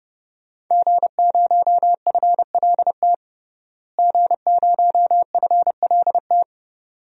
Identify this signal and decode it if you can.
Morse code, heard as a keyed tone, about 30 wpm, 700 Hz. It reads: G0FLT G0FLT
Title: DXCC-100 Call Signs - 5 Characters - Repeated 2x - 30wpm